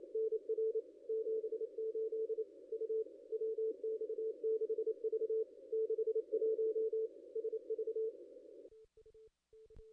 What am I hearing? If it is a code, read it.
Morse code, heard as a keyed tone, about 28 wpm, 410 Hz. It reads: NR 78 U WX6V 61 SV
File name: contests / ss-cw-k5tr-op-n5rz -0600